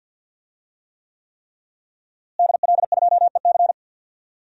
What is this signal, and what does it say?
Morse code, heard as a keyed tone, about 52 wpm, 700 Hz. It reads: DL2EC
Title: DXCC-100 Call Signs - 5 Characters - Repeated 3x - 50wpm